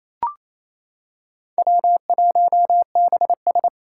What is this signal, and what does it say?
Morse code, heard as a keyed tone, about 28 wpm, 700 Hz. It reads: W1BS
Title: Four Challenging Letter-Number Combinations 28wpm